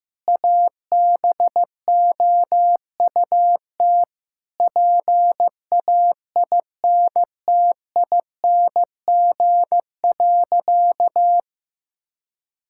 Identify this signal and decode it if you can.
Morse code, heard as a keyed tone, about 15 wpm, 700 Hz. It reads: ABOUT PAINTING.
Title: Sentences from Top 1000 Words - Review Part 6 - 15wpm